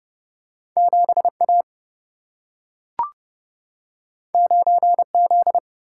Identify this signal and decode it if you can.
Morse code, heard as a keyed tone, about 30 wpm, 700 Hz. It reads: 7A 9Z
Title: Single-Letter Single-Number 30wpm